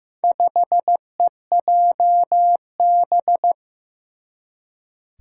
Morse code, heard as a keyed tone, about 15 wpm, 700 Hz. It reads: 5EJB
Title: Four Challenging Letter-Number Combinations 15wpm